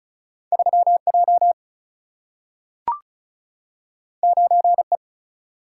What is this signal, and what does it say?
Morse code, heard as a keyed tone, about 36 wpm, 700 Hz. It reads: 3J 9E
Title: Single-Letter Single-Number 35wpm